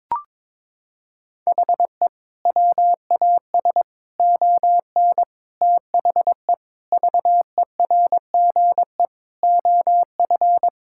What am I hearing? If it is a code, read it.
Morse code, heard as a keyed tone, about 22 wpm, 700 Hz. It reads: HE WAS ON THE VERGE OF
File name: practice / 22wpm / Sentences with 4-Word Phrases - 22wpm